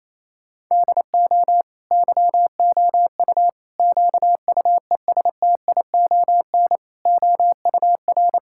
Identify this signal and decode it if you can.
Morse code, heard as a keyed tone, about 27 wpm, 700 Hz. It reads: DO YOU QUESTION OUR
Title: Sentences from Top 1000 Words - Review Part 5 - 28wpm